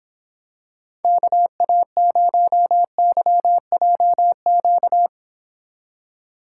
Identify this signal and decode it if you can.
Morse code, heard as a keyed tone, about 26 wpm, 700 Hz. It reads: KA0YJQ